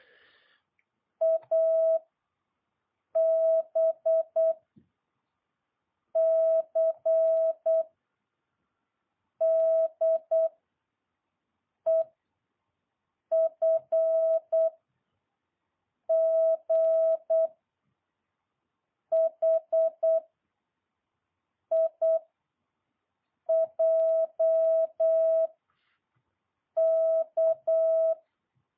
Morse code, heard as a keyed tone, about 8 wpm, 650 Hz. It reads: A B C D E F G H I J K
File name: jh8cbh